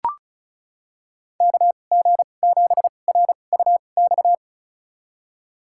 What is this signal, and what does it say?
Morse code, heard as a keyed tone, about 34 wpm, spaced as for 25 wpm, 700 Hz. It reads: KG7RUX
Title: USA Call Signs - 6 Characters - 2x Character Spacing - 35wpm